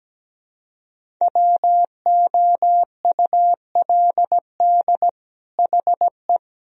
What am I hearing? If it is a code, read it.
Morse code, heard as a keyed tone, about 17 wpm, 700 Hz. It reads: WOULD HE
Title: Sentences from Top 1000 Words - Review Part 1 - 17wpm